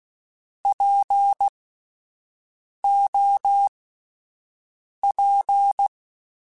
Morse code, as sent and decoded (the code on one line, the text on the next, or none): .--. --- .--.
POP